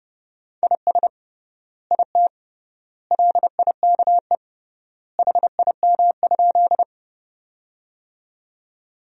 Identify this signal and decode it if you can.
Morse code, heard as a keyed tone, about 30 wpm, 700 Hz. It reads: IS IT LIKE HIM?
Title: 3x Word-Spacing - Sentences from Top 100 Words - 30wpm